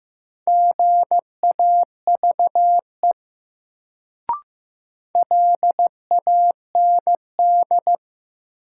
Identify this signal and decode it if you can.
Morse code, heard as a keyed tone, about 15 wpm, 700 Hz. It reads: GAVE LAND
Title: Sentences from Top 500 Words - New Word Drill - 15wpm